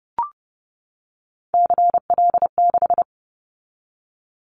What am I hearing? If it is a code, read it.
Morse code, heard as a keyed tone, about 30 wpm, 700 Hz. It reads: CL6
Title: Callsign Prefixes for Top 100 DXCC - 30wpm